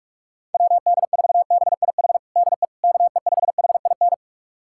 Morse code, heard as a keyed tone, about 45 wpm, 700 Hz. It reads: WD4BIH DE KE5HIN
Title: USA Callsign DE Callsign 45wpm